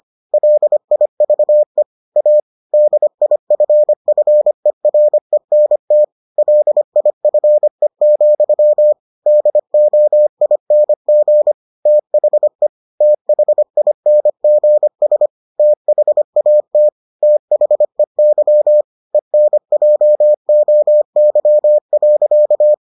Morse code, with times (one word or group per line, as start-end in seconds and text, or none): 0.3-1.8 LIVE
2.2-2.4 A
2.7-6.0 DIFFERENT
6.4-8.9 LIFE,
9.3-11.5 DOING
11.9-12.7 THE
13.0-15.3 THINGS
15.6-16.9 THAT
17.2-18.8 THEY
19.1-22.7 ENJOY.